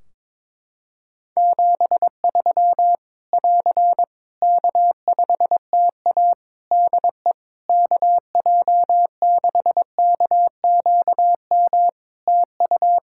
Morse code, heard as a keyed tone, about 22 wpm, 700 Hz. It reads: 73 <AR> K5TA DE KJ6KQM TU
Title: Practice QSO - Exchange 4 - Ending - 22wpm